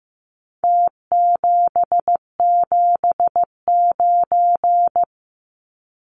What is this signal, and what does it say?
Morse code, heard as a keyed tone, about 15 wpm, 700 Hz. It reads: T779
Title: Numbers - Contest Style - 4-Digit 15wpm